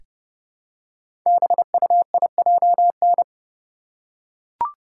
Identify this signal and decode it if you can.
Morse code, heard as a keyed tone, about 30 wpm, 700 Hz. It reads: BUIJN
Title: Code Groups - 5 Letters - 30wpm